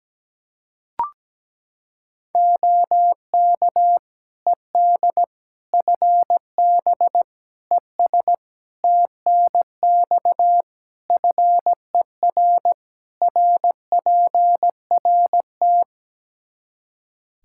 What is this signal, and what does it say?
Morse code, heard as a keyed tone, about 17 wpm, 700 Hz. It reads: OK ED FB ES TNX FER RPRT
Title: K1USN SST QSO Practice 17wpm